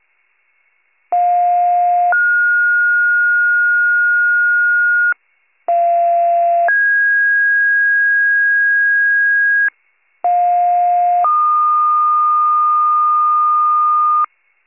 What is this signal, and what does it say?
Two-tone page, heard as a one-second tone, then a three-second tone, about 710 then 1430 Hz, about 690 then 1650 Hz, about 710 then 1220 Hz, with noise. It is heard from 1.1 to 14.3 s.